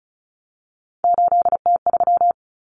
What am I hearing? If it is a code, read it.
Morse code, heard as a keyed tone, about 34 wpm, 700 Hz. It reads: 8T3
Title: Common Cut Numbers - 3-Digit 35wpm